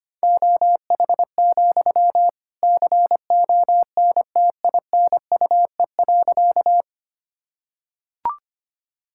Morse code, heard as a keyed tone, about 25 wpm, 700 Hz. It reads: OH, CONTINUE.
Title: Sentences from Top 1000 Words - Review Part 6 - 25wpm